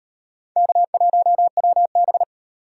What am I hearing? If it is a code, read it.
Morse code, heard as a keyed tone, about 38 wpm, 700 Hz. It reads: K1WB